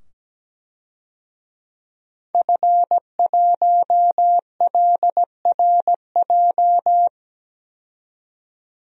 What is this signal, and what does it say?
Morse code, heard as a keyed tone, about 17 wpm, 700 Hz. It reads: F1LRJ
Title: DXCC-100 Call Signs - 5 Characters - 17wpm